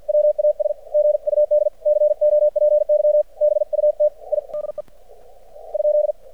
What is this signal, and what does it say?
Morse code, heard as a keyed tone, about 50 wpm, 600 Hz. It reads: WAS GUD COPY BUT I DE ?
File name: archived / CRSnet